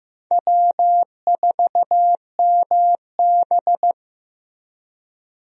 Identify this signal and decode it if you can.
Morse code, heard as a keyed tone, about 15 wpm, 700 Hz. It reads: W4MB
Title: USA Call Signs - 4 Characters - Repeated 2x - 15wpm